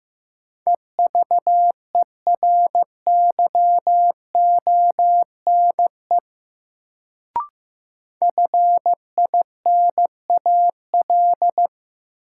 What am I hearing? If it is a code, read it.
Morse code, heard as a keyed tone, about 15 wpm, 700 Hz. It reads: EVERYONE FINAL
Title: Sentences from Top 800 Words - New Word Drill - 15wpm